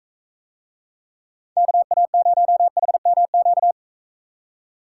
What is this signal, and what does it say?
Morse code, heard as a keyed tone, about 40 wpm, 700 Hz. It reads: KA0HMQ